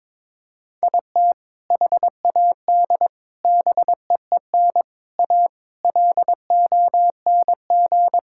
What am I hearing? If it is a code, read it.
Morse code, heard as a keyed tone, about 22 wpm, 700 Hz. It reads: IT HAD BEEN A LONG
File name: practice / 22wpm / Sentences with 3-Word Phrases - 22wpm